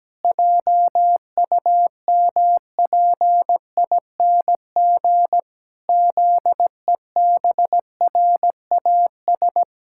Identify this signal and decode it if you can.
Morse code, heard as a keyed tone, about 17 wpm, 700 Hz. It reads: JUMPING ZEBRAS
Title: Pangram Challenge v2 - 17wpm